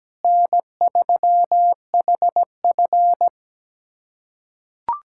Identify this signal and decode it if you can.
Morse code, heard as a keyed tone, about 17 wpm, 700 Hz. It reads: N3HF
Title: USA Callsign DE Callsign - 4 Characters - 17wpm